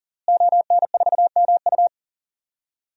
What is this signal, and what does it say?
Morse code, heard as a keyed tone, about 37 wpm, 700 Hz. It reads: ON4MU
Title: DXCC-100 Call Signs - 5 Characters - Repeated 3x - 40wpm